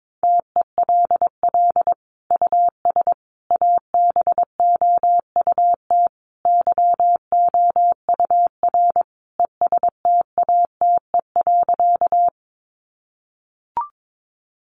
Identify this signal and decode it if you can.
Morse code, heard as a keyed tone, about 22 wpm, 700 Hz. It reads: TELL US ABOUT YOUR ESTATE.
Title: Sentences from Top 1200 Words 22wpm